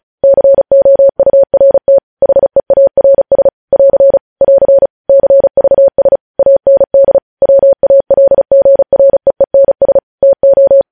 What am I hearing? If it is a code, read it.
Morse code, heard as a keyed tone, about 36 wpm, 550 Hz. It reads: COURT HEARS <AR> <AR> CVS AND WALGREENS TO
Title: News Headlines at 35